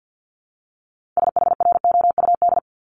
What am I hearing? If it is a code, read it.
Morse code, heard as a keyed tone, about 50 wpm, 700 Hz. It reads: SHROUD
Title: Multi-letter Phonemes Repeated 3x 50wpm